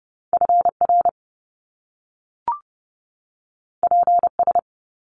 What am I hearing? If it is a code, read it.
Morse code, heard as a keyed tone, about 30 wpm, 700 Hz. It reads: FR PS
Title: Top Two Letter Combinations in Words 30wpm